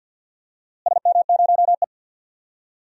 Morse code, heard as a keyed tone, about 51 wpm, 700 Hz. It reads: SM0E
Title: DXCC-100 Call Signs - 4 Characters - Repeated 3x - 50wpm